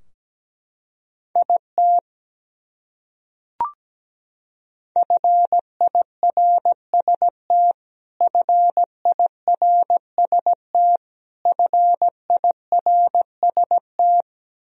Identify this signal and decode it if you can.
Morse code, heard as a keyed tone, about 17 wpm, 700 Hz. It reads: IT FIRST FIRST FIRST
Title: Top 100 Words Repeated 17wpm